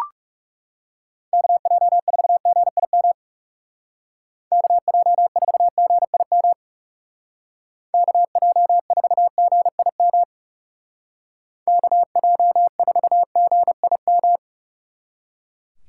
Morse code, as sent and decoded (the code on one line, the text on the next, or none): -.- .--- ....- --. .. -- / -.- .--- ....- --. .. -- / -.- .--- ....- --. .. -- / -.- .--- ....- --. .. --
KJ4GIM KJ4GIM KJ4GIM KJ4GIM